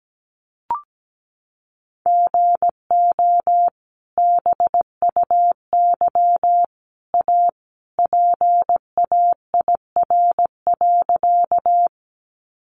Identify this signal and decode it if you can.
Morse code, heard as a keyed tone, about 17 wpm, 700 Hz. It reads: GO BUY A PAIR.
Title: Sentences from Top 1200 Words 17wpm